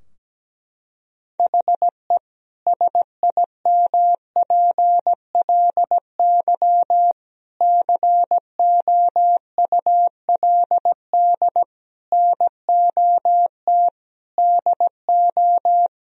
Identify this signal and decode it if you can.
Morse code, heard as a keyed tone, about 17 wpm, 700 Hz. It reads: HE SIMPLY COULD NOT DO
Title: Sentences from Top 1000 Words - Review Part 4 - 17wpm